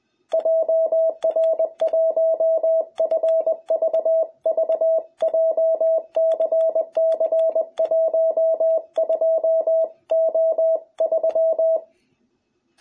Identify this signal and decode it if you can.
Morse code, heard as a keyed tone, about 20 wpm, 660 Hz, with radio noise. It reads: JR1FVV JCC12O3
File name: jh8cbh / 2021cw